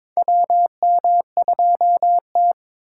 Morse code, heard as a keyed tone, about 22 wpm, 700 Hz. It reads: WM2T
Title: USA Call Signs - 4 Characters - 3x Character Spacing - 22wpm